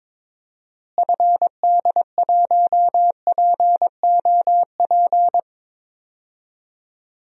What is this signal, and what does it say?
Morse code, heard as a keyed tone, about 22 wpm, 700 Hz. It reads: FD1POP
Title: DXCC-100 Call Signs - 6 Characters - Repeated 2x - 22wpm